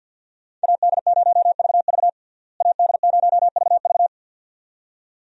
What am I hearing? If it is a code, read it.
Morse code, heard as a keyed tone, about 48 wpm, 700 Hz. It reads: AD0VV AD0VV
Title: USA Call Signs - 5 Characters - Repeated 2x - 50wpm